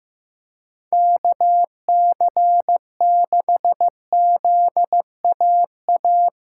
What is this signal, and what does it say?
Morse code, heard as a keyed tone, about 15 wpm, 700 Hz. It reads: KC6ZAA
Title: USA Callsign DE Callsign - 6 Characters - 15wpm